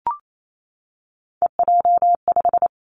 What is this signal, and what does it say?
Morse code, heard as a keyed tone, about 28 wpm, 700 Hz. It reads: EJ5
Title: Three Challenging Letter-Number Combinations 28wpm